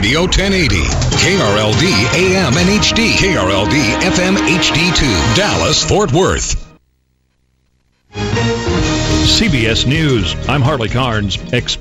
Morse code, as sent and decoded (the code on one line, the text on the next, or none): none